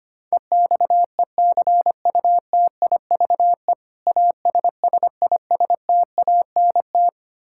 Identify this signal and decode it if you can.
Morse code, heard as a keyed tone, about 25 wpm, 700 Hz. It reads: EXECUTIVE ASSISTANT
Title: Practice QSO - Exchange 3 - More Info - 25wpm